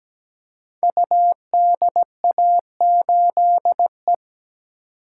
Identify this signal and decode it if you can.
Morse code, heard as a keyed tone, about 17 wpm, 700 Hz. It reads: UDA8E
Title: Code Groups - 5 Letter:Numbers - 17wpm